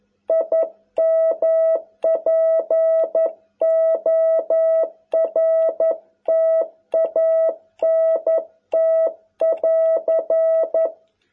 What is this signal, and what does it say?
Morse code, heard as a keyed tone, about 11 wpm, 660 Hz. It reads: IMPORTANT <AR>